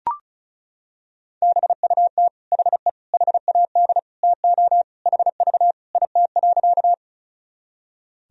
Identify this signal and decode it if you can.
Morse code, heard as a keyed tone, about 34 wpm, 700 Hz. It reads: BUT HE HAD TO HV IT.
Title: Sentences from Top 100 Words and Abbreviations 35wpm